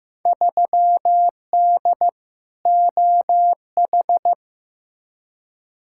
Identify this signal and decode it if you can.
Morse code, heard as a keyed tone, about 15 wpm, 700 Hz. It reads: 3D OH
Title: Field Day Exchanges - 15wpm